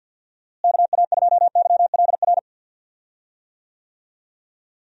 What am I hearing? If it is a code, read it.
Morse code, heard as a keyed tone, about 47 wpm, 700 Hz. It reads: KA2YLR
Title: USA Call Signs - 6 Characters - Repeated 3x - 50wpm